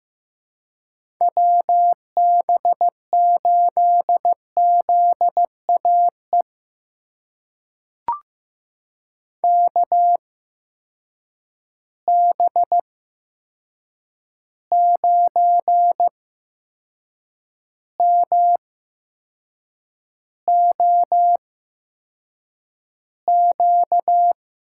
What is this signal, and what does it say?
Morse code, heard as a keyed tone, about 15 wpm, spaced as for 6 wpm, 700 Hz. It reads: WB8ZAE KB9MOQ
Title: USA Call Signs - 6 Characters - 8x Character Spacing - 15wpm